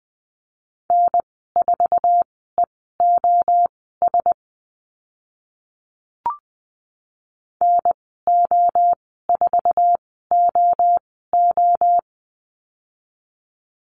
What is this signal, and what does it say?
Morse code, heard as a keyed tone, about 20 wpm, spaced as for 14 wpm, 700 Hz. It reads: N4EOS NO4OO
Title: 055 Lesson 10g - Letter r - Call Signs 20wpm